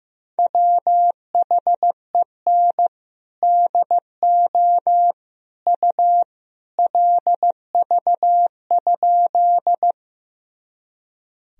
Morse code, heard as a keyed tone, about 16 wpm, 700 Hz. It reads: WHEN DO U LV?
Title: Sentences from Top 400 Words and Abbreviations 15wpm